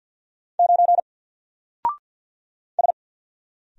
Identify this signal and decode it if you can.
Morse code, heard as a keyed tone, about 48 wpm, 700 Hz. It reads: 9 S
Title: Single Letter-Number Alt 50wpm